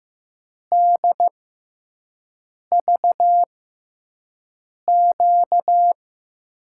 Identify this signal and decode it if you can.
Morse code, heard as a keyed tone, about 15 wpm, spaced as for 5 wpm, 700 Hz. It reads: DVQ
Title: USA Call Signs - 5 Characters - 6x Character Spacing - 15wpm